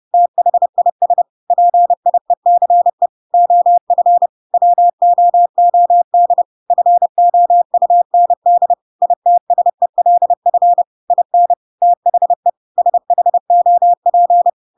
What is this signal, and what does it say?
Morse code, heard as a keyed tone, about 30 wpm, 700 Hz. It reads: THIS PIECE OF WOOD FOUND ITSELF IN THE SHOP
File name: Pinocchio / Pinocchio ch1 30 30 ik1hge